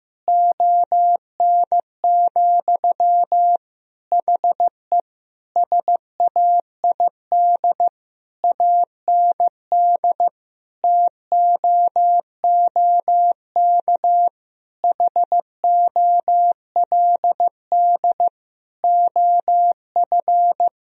Morse code, heard as a keyed tone, about 15 wpm, 700 Hz. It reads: ON, HE SAID AND TOOK HOLD OF
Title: Sentences from Top 1000 Words - Review Part 2 - 15wpm